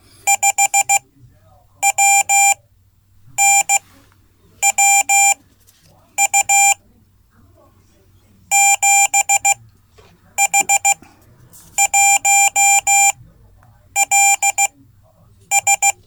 Morse code, heard as a keyed tone, about 15 wpm, spaced as for 8 wpm, 790 Hz. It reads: 5WNWU 7H1LS
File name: Code / 5min-c